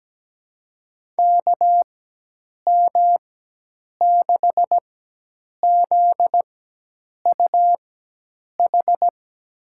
Morse code, heard as a keyed tone, about 17 wpm, spaced as for 8 wpm, 700 Hz. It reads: KM6ZUH
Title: USA Call Signs - 6 Characters - 4x Character Spacing - 17wpm